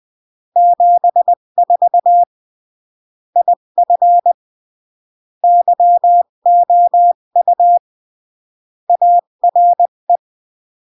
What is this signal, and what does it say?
Morse code, heard as a keyed tone, about 20 wpm, 700 Hz. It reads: 74 IF YOU ARE
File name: CW / jodorowsky / 82 commandments 20-15